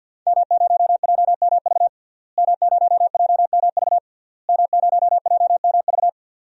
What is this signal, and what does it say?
Morse code, heard as a keyed tone, about 49 wpm, 700 Hz. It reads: M0JMV M0JMV M0JMV
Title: DXCC-100 Call Signs - 5 Characters - Repeated 3x - 50wpm